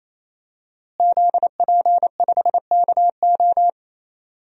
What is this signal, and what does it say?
Morse code, heard as a keyed tone, about 28 wpm, 700 Hz. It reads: ZP5KO
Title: DXCC-100 Call Signs - 5 Characters - Repeated 3x - 28wpm